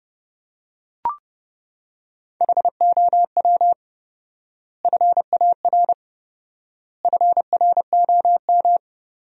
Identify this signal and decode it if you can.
Morse code, heard as a keyed tone, about 30 wpm, 700 Hz. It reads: HOW FAR FROM
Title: 4x Word-Spacing - Sentences from Top 400 Words - 30wpm